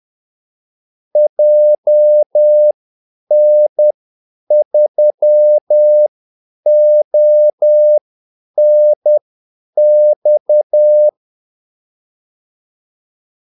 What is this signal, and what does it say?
Morse code, heard as a keyed tone, about 10 wpm, spaced as for 8 wpm, 600 Hz. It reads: JN3ONX
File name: Callsigns-10-8wpm